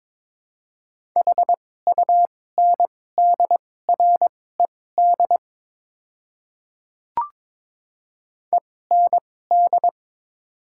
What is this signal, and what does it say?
Morse code, heard as a keyed tone, about 22 wpm, spaced as for 16 wpm, 690 Hz. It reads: HUNDRED END
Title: 2x Character Spacing - Top 500 Words - 22wpm